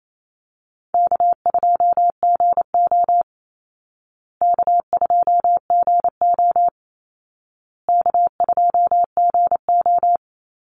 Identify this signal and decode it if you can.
Morse code, heard as a keyed tone, about 28 wpm, 700 Hz. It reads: K2GO K2GO K2GO
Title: Easy Call Signs Repeated 3x - 28wpm